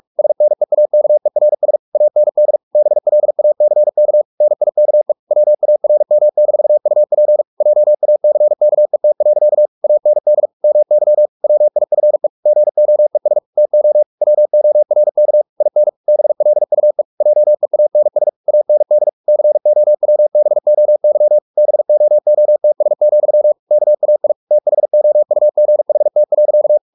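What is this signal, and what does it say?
Morse code, heard as a keyed tone, about 44 wpm, 600 Hz. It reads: SNEAKERS AND BLACK NIKE WARM-UP JACKET. AND MY WIFE GOES TO WORK IN BLUE JEANS AND COWBOY BOOTS, KAI THOUGHT.
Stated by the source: Morse practice at 45 wpm, the novel Never by Ken Follett in English, from CW med Gnister.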